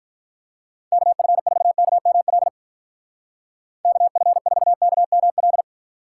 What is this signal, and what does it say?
Morse code, heard as a keyed tone, about 44 wpm, 700 Hz. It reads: KF4KML KF4KML